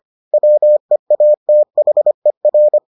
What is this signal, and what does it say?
Morse code, heard as a keyed tone, about 25 wpm, 600 Hz. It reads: WEATHER